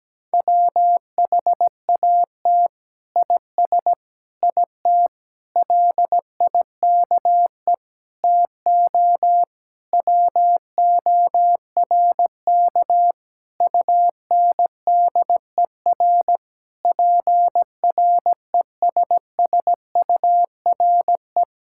Morse code, heard as a keyed tone, about 17 wpm, 700 Hz. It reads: WHAT IS IT LIKE TO WORK UNDER PRESSURE
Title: Sentences from Top 1000 Words - Review Part 6 - 17wpm